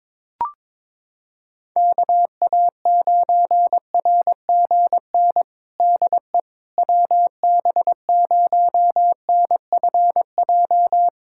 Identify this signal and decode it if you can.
Morse code, heard as a keyed tone, about 22 wpm, 700 Hz. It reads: KA9RGN DE WB0NFJ
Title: USA Callsign DE Callsign - 6 Characters - 22wpm